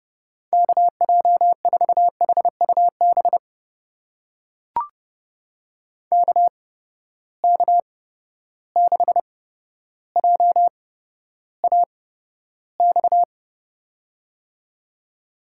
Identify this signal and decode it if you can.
Morse code, heard as a keyed tone, about 31 wpm, spaced as for 13 wpm, 700 Hz. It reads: KJ4HUB KK6JAX
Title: USA Call Signs - 6 Characters - 8x Character Spacing - 30wpm